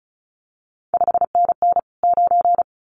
Morse code, heard as a keyed tone, about 36 wpm, 700 Hz. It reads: 5NN 9
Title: CQ WW DX Contest Exchange - 35wpm